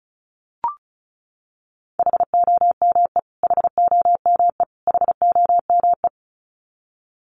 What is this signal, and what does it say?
Morse code, heard as a keyed tone, about 34 wpm, 700 Hz. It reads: HOME HOME HOME
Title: Top 500 Words Repeated 35wpm